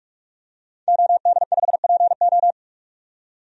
Morse code, heard as a keyed tone, about 41 wpm, 700 Hz. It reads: OD5PO
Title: DXCC-100 Call Signs - 5 Characters - 45wpm